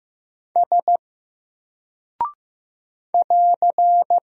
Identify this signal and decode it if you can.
Morse code, heard as a keyed tone, about 15 wpm, 700 Hz. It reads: S <AR>
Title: Single Character (Letter:Number:Punctuation:Prosign) 15wpm